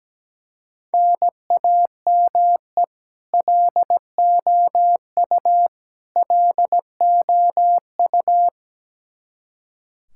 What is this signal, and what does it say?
Morse code, heard as a keyed tone, about 17 wpm, 700 Hz. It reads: NAME LOU LOU
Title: Practice QSO - Exchange 1 - Beginning - 17wpm